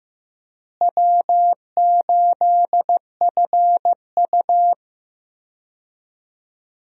Morse code, heard as a keyed tone, about 15 wpm, 700 Hz. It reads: W8FU
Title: USA Call Signs - 4 Characters - Repeated 2x - 15wpm